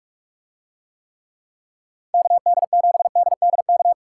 Morse code, heard as a keyed tone, about 46 wpm, 700 Hz. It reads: KD7DDK